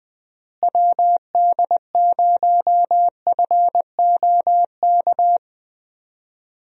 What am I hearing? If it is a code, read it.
Morse code, heard as a keyed tone, about 20 wpm, 700 Hz. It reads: WD0FOK
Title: USA Call Signs - 6 Characters - Repeated 3x - 20wpm